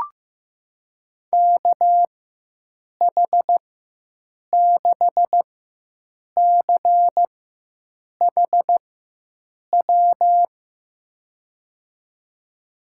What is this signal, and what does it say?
Morse code, heard as a keyed tone, about 15 wpm, spaced as for 7 wpm, 700 Hz. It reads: KH6CHW
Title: USA Call Signs - 6 Characters - 4x Character Spacing - 15wpm